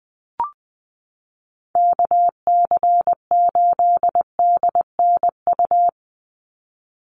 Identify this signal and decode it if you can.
Morse code, heard as a keyed tone, about 20 wpm, 700 Hz. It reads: KC8DNU
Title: USA Call Signs 20wpm